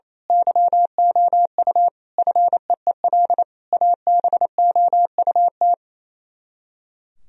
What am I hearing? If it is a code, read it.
Morse code, heard as a keyed tone, about 28 wpm, 700 Hz. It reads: YOU FEEL ABOUT
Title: Speed-Racing - Sets of 5 Words 28wpm